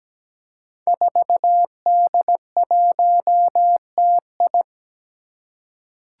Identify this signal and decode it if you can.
Morse code, heard as a keyed tone, about 17 wpm, 700 Hz. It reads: 4D1TI
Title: Five Challenging Letter-Number Combinations 17wpm